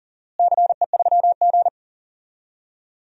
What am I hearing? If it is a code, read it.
Morse code, heard as a keyed tone, about 40 wpm, 700 Hz. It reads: CE3G
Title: DXCC-100 Call Signs - 4 Characters - Repeated 2x - 40wpm